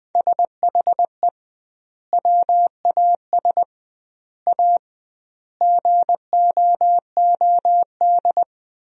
Morse code, heard as a keyed tone, about 21 wpm, 700 Hz. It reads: SHE WAS A GOOD
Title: 2x Word-Spacing - Sentences from Top 400 Words - 20wpm